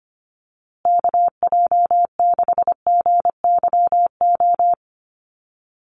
Morse code, heard as a keyed tone, about 25 wpm, 700 Hz. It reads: KJ6GYO
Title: USA Call Signs Repeated 2x 25wpm